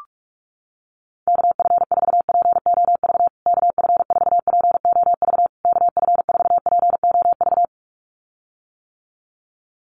Morse code, heard as a keyed tone, about 46 wpm, 700 Hz. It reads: KF4POV KF4POV KF4POV
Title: USA Call Signs Repeated 3x 45wpm